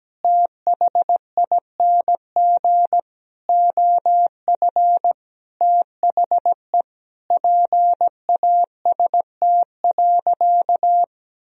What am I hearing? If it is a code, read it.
Morse code, heard as a keyed tone, about 17 wpm, 700 Hz. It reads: THING OF THE PAST.
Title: Sentences from Top 1000 Words - Review Part 3 - 17wpm